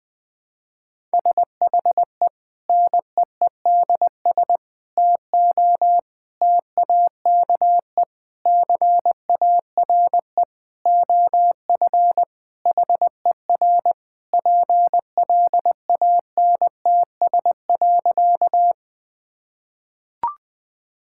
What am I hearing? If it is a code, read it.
Morse code, heard as a keyed tone, about 21 wpm, 700 Hz. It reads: SHE NEEDS TO TAKE CARE OF HER PLANTS.
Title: Sentences with 4-Word Phrases - 20wpm